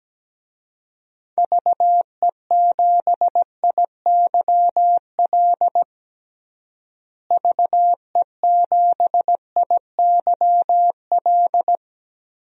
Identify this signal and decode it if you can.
Morse code, heard as a keyed tone, about 17 wpm, 700 Hz. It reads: VE7IYL VE7IYL
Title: DXCC-100 Call Signs - 6 Characters - Repeated 3x - 17wpm